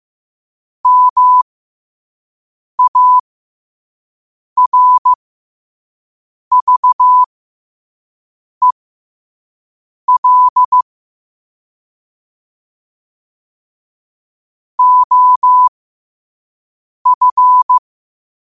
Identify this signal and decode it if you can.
Morse code, heard as a keyed tone, about 15 wpm, spaced as for 5 wpm, 1000 Hz. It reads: MARVEL OF
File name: qotdfiles / 5wpm